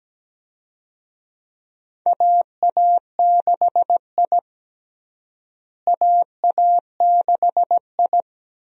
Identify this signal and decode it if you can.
Morse code, heard as a keyed tone, about 17 wpm, 700 Hz. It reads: AA6I AA6I
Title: USA Call Signs - 4 Characters - Repeated 3x - 17wpm